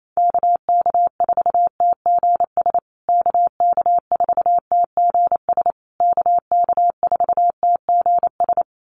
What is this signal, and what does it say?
Morse code, heard as a keyed tone, about 28 wpm, 700 Hz. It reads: KK4TGS KK4TGS KK4TGS
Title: USA Call Signs Repeated 3x 28wpm